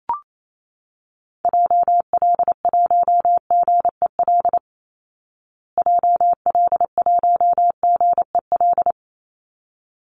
Morse code, heard as a keyed tone, about 28 wpm, 690 Hz. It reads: JL1GEL JL1GEL
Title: Call Signs Repeated 3x - 28wpm